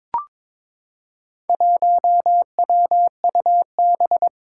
Morse code, heard as a keyed tone, about 22 wpm, 700 Hz. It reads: 1WUB
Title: Four Challenging Letter-Number Combinations 22wpm